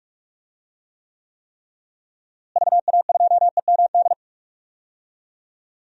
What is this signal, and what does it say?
Morse code, heard as a keyed tone, about 44 wpm, 700 Hz. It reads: VA2EMD
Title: DXCC-100 Call Signs - 6 Characters - Repeated 3x - 45wpm